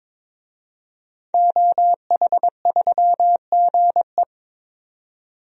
Morse code, heard as a keyed tone, about 22 wpm, 700 Hz. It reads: OH3GE
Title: DXCC-100 Call Signs - 5 Characters - 22wpm